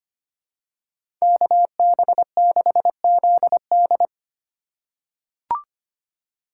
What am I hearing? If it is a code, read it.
Morse code, heard as a keyed tone, about 25 wpm, 700 Hz. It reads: KB6ZD
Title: USA Call Signs - 4x Character Spacing - 25wpm